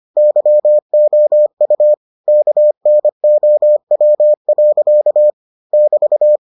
Morse code, heard as a keyed tone, about 25 wpm, 600 Hz. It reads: YOU KNOW. <BT>